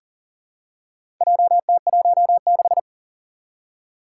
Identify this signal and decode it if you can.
Morse code, heard as a keyed tone, about 39 wpm, 700 Hz. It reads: JT16
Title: Four Challenging Letter-Number Combinations 40wpm